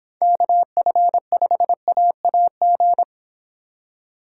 Morse code, heard as a keyed tone, about 26 wpm, 700 Hz. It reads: KF5AAG